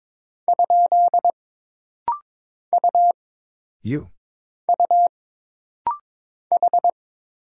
Morse code, heard as a keyed tone, about 22 wpm, 700 Hz. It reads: ? U U H
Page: This rapid-fire timing is geared to push your ICR (Instant Character Recognition) skills.